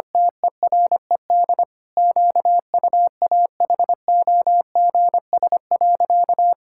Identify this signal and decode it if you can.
Morse code, heard as a keyed tone, about 25 wpm, 700 Hz. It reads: TERED QUAHOGS.
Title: Pangram Challenge v2 - 25wpm